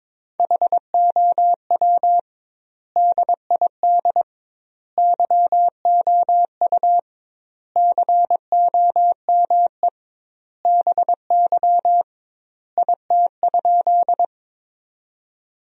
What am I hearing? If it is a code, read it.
Morse code, heard as a keyed tone, about 22 wpm, 700 Hz. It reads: HOW DID YOU COME BY IT?
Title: 2x Word-Spacing - Sentences from Top 100 Words - 22wpm